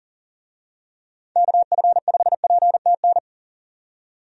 Morse code, heard as a keyed tone, about 40 wpm, 700 Hz. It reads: KF5PTN
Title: USA Call Signs - 6 Characters - 40wpm